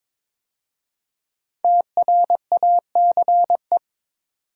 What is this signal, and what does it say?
Morse code, heard as a keyed tone, about 22 wpm, 700 Hz. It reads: TRACE
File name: 5 Letter Words - First 3 Letters Given - 22wpm